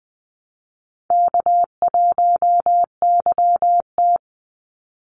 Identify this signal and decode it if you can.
Morse code, heard as a keyed tone, about 20 wpm, 700 Hz. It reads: K1YT
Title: Call Signs Repeated 3x - 20wpm